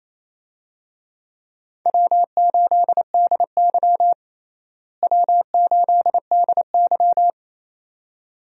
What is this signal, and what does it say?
Morse code, heard as a keyed tone, about 28 wpm, 700 Hz. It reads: W8DY W8DY
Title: USA Call Signs - 4 Characters - Repeated 2x - 28wpm